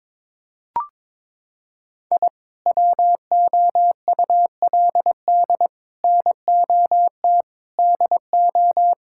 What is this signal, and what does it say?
Morse code, heard as a keyed tone, about 22 wpm, 700 Hz. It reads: I WOULD NOT DO
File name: practice / 22wpm / Sentences with 3-Word Phrases - 22wpm